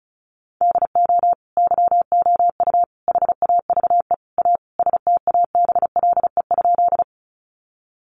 Morse code, heard as a keyed tone, about 34 wpm, 700 Hz. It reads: DO YOU HAVE A STABLE?
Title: Sentences from Top 1500 Words 35wpm